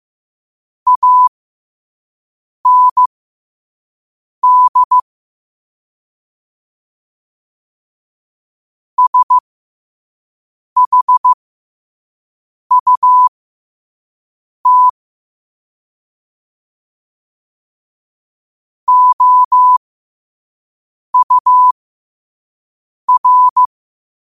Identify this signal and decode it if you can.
Morse code, heard as a keyed tone, about 15 wpm, spaced as for 5 wpm, 1000 Hz. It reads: AND SHUT OUR
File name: qotdfiles / 5wpm